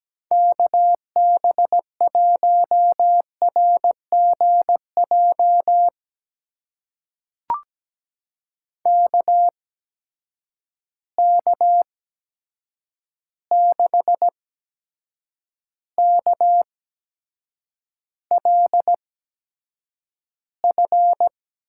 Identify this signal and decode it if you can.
Morse code, heard as a keyed tone, about 17 wpm, spaced as for 7 wpm, 700 Hz. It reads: KB1RGJ KK6KLF
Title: USA Call Signs - 6 Characters - 8x Character Spacing - 17wpm